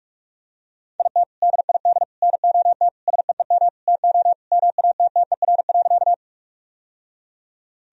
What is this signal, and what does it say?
Morse code, heard as a keyed tone, about 46 wpm, 700 Hz. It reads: IT DID NOT SEEM TO MATTER.
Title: Sentences from Top 1000 Words - Review Part 1 - 45wpm